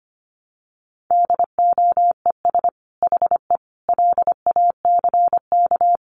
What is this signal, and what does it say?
Morse code, heard as a keyed tone, about 25 wpm, 700 Hz. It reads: DOES HE LACK
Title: Sentences from Top 1100 Words 25wpm